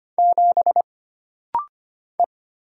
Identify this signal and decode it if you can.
Morse code, heard as a keyed tone, about 25 wpm, 700 Hz. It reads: 7 E
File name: Single Character (Letter:Number:Punctuation:Prosign) - Rapid-Fire - 25wpm